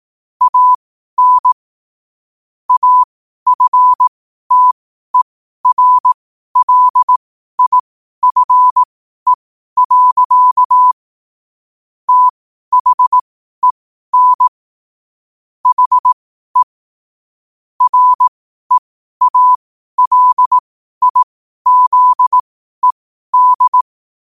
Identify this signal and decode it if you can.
Morse code, heard as a keyed tone, about 18 wpm, spaced as for 12 wpm, 1000 Hz. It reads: AN AFTERLIFE. THEN HE REALIZED
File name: qotdfiles / 12wpm